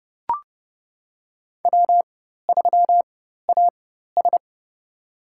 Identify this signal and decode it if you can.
Morse code, heard as a keyed tone, about 30 wpm, spaced as for 14 wpm, 700 Hz. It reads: W3AS
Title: USA Call Signs - 4 Characters - 4x Character Spacing - 30wpm